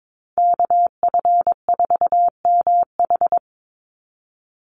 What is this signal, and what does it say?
Morse code, heard as a keyed tone, about 22 wpm, 700 Hz. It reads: KF4MH
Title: NA SOTA Chaser Callsigns 22wpm